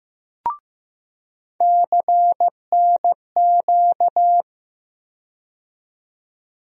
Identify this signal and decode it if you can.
Morse code, heard as a keyed tone, about 15 wpm, 700 Hz. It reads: CNQ
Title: USA Call Sign Postfixes - 3 Letters 15wpm